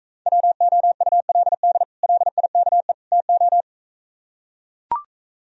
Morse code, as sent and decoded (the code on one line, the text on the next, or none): .-- --- ..- .-.. -.. / .-.. .. -.- . / - ---
WOULD LIKE TO